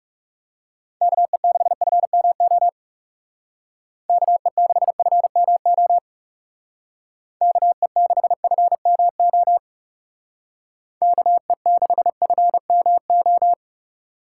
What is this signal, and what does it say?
Morse code, heard as a keyed tone, about 39 wpm, 700 Hz. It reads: KE6FMO KE6FMO KE6FMO KE6FMO